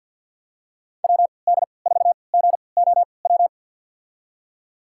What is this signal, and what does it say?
Morse code, heard as a keyed tone, about 52 wpm, spaced as for 28 wpm, 690 Hz. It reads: WD4GOW
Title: USA Call Signs - 3x Character Spacing - 50wpm